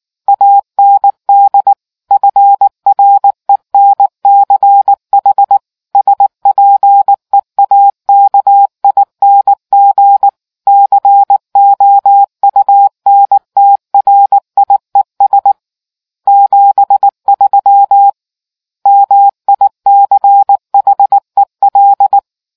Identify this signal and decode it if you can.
Morse code, heard as a keyed tone, about 19 wpm, 800 Hz. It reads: AND FRENCH SPEAKING COUNTRIES 73 MICHEL